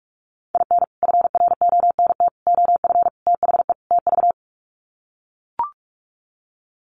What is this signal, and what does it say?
Morse code, heard as a keyed tone, about 44 wpm, 700 Hz. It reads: IN FRONT OF THE TV
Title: Sets of 5 Words Encore2 45wpm